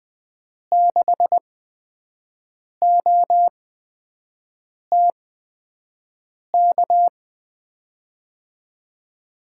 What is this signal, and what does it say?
Morse code, heard as a keyed tone, about 20 wpm, spaced as for 5 wpm, 700 Hz. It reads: 6OTK
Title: USA Call Signs - 6 Characters - 8x Character Spacing - 20wpm